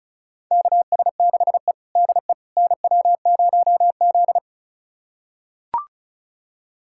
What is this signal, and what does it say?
Morse code, heard as a keyed tone, about 36 wpm, 700 Hz. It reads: KS6E DE NW0Z
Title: USA Callsign DE Callsign - 4 Characters - 35wpm